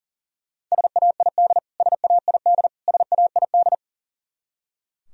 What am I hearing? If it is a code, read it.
Morse code, heard as a keyed tone, about 37 wpm, 700 Hz. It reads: SAID SAID SAID
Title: Top 100 Words Repeated 40wpm